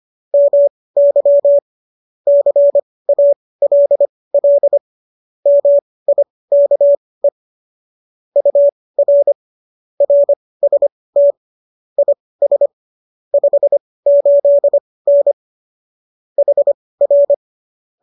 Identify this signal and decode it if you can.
Morse code, heard as a keyed tone, about 25 wpm, spaced as for 18 wpm, 570 Hz. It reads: MY CALL MIKE UR RST IS 58N HR